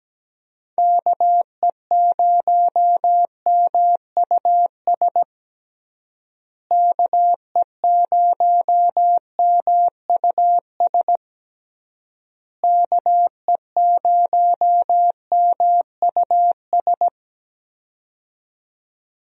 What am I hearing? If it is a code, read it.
Morse code, heard as a keyed tone, about 17 wpm, 700 Hz. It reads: KE0MUS KE0MUS KE0MUS
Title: USA Call Signs - 6 Characters - Repeated 3x - 17wpm